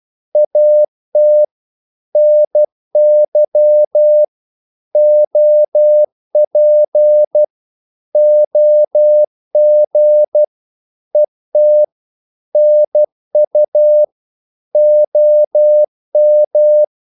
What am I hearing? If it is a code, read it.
Morse code, heard as a keyed tone, about 12 wpm, 600 Hz. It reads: AT NY OP OG ET NU OM
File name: Korte ord DK 12wpm